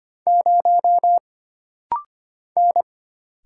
Morse code, heard as a keyed tone, about 25 wpm, 700 Hz. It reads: 0 N
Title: Single Letter-Number Alt - Rapid-Fire 25wpm